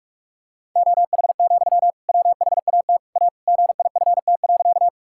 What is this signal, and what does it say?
Morse code, heard as a keyed tone, about 46 wpm, 700 Hz. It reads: OH, WHAT A GIFT.
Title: Sentences from Top 1000 Words - Review Part 6 - 45wpm